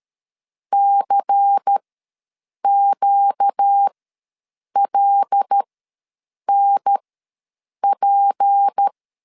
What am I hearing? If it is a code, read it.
Morse code, heard as a keyed tone, about 13 wpm, spaced as for 7 wpm, 800 Hz. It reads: CQLNP
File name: LEZIONI / 7 GF9Z / 7 GF9 ws 12wpm 3344